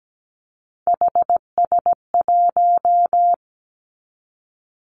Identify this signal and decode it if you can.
Morse code, heard as a keyed tone, about 17 wpm, 700 Hz. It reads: HS1
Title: Three Challenging Letter-Number Combinations 17wpm